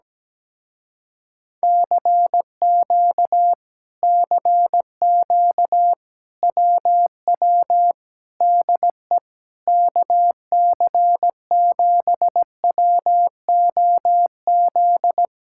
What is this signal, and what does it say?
Morse code, heard as a keyed tone, about 17 wpm, 700 Hz. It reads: CQ CQ WW DE KC7WOZ
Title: Speed-Racing Common QSO Phrases - 17wpm